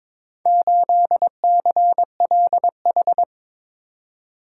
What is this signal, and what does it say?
Morse code, heard as a keyed tone, about 22 wpm, 700 Hz. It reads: 8CLH
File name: practice / 22wpm / Four Letter-Number Combinations - 22wpm